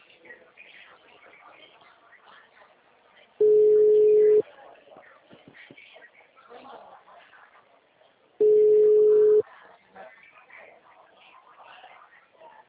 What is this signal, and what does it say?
Ringback tone (Europe, 425 Hz), with speech.